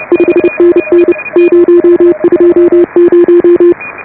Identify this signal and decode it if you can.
Morse code, heard as a keyed tone, about 30 wpm, 340 Hz. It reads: 5NN 020